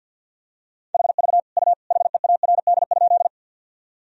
Morse code, heard as a keyed tone, about 50 wpm, 700 Hz. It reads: HV U HEARD?
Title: Sentences from Top 400 Words and Abbreviations 50wpm